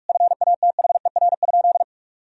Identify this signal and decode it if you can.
Morse code, heard as a keyed tone, about 41 wpm, 700 Hz. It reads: FATHER?
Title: Sentences from Top 1000 Words - Review Part 2 - 45wpm